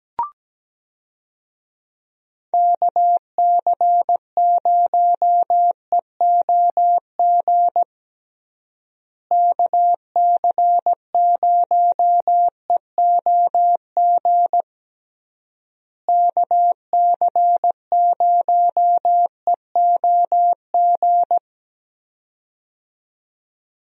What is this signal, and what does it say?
Morse code, heard as a keyed tone, about 17 wpm, 700 Hz. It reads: KC0EOG KC0EOG KC0EOG
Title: USA Call Signs - 6 Characters - Repeated 3x - 17wpm